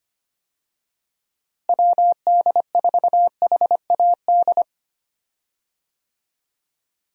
Morse code, heard as a keyed tone, about 25 wpm, 700 Hz. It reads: WD4HAD
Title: USA Call Signs - 6 Characters - Repeated 3x - 25wpm